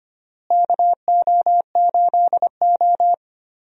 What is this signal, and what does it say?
Morse code, heard as a keyed tone, about 25 wpm, 700 Hz. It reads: KO8O
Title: USA Call Signs - 4 Characters - Repeated 2x - 25wpm